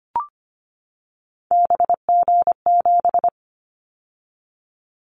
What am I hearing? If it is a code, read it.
Morse code, heard as a keyed tone, about 25 wpm, 700 Hz. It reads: BG7
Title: Callsign Prefixes for Top 100 DXCC - 25wpm